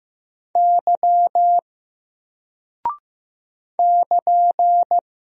Morse code, heard as a keyed tone, about 15 wpm, 700 Hz. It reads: Y <KN>
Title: Single Character (Letter:Number:Punctuation:Prosign) 15wpm